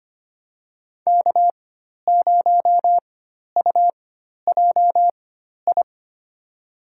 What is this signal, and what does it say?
Morse code, heard as a keyed tone, about 25 wpm, spaced as for 12 wpm, 700 Hz. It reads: K0UJI
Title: USA Call Signs - 5 Characters - 4x Character Spacing - 25wpm